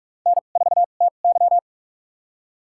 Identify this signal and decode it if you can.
Morse code, heard as a keyed tone, about 46 wpm, spaced as for 32 wpm, 700 Hz. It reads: N4TY
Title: USA Call Signs - 4 Characters - 2x Character Spacing - 45wpm